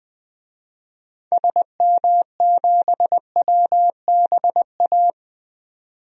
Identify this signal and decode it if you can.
Morse code, heard as a keyed tone, about 20 wpm, 700 Hz. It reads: SM7WBA
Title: DXCC-100 Call Signs - 6 Characters - 20wpm